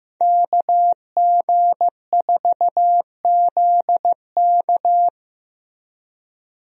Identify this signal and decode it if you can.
Morse code, heard as a keyed tone, about 15 wpm, 700 Hz. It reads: KG4ZK
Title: USA Call Signs - 5 Characters - Repeated 3x - 15wpm